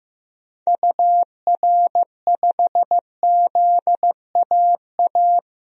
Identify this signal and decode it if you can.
Morse code, heard as a keyed tone, about 15 wpm, 700 Hz. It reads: UR5ZAA
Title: DXCC-100 Call Signs - 6 Characters - 15wpm